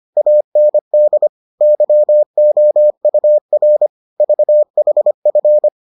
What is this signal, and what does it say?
Morse code, heard as a keyed tone, about 25 wpm, 600 Hz. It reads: AND YOUR VHF